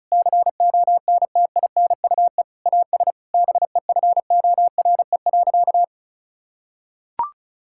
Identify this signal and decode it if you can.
Morse code, heard as a keyed tone, about 36 wpm, 700 Hz. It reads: CONTINUE AS BEFORE.
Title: Sentences from Top 1000 Words - Review Part 6 - 35wpm